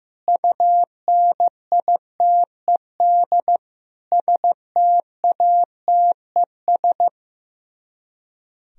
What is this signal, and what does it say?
Morse code, heard as a keyed tone, about 16 wpm, 700 Hz. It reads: UNITED STATES
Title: Speed-Racing - Sets of 4 Words 15wpm